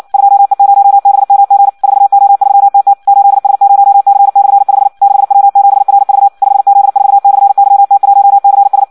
Morse code, heard as a keyed tone, about 56 wpm, 800 Hz. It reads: 2E0BMG 4O3TT 8S0C/5 6W7RV 5Z4/YT1CS
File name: Audio practica CW / qrq-tx-58 2